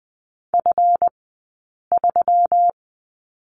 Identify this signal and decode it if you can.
Morse code, heard as a keyed tone, about 20 wpm, spaced as for 8 wpm, 700 Hz. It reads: F3
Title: 165 Lesson 27d - Number 7 - Two Characters 20wpm